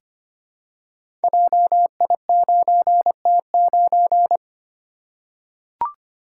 Morse code, heard as a keyed tone, about 25 wpm, 700 Hz. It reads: JI9T9
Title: Code Groups - 5 Letter:Numbers - 25wpm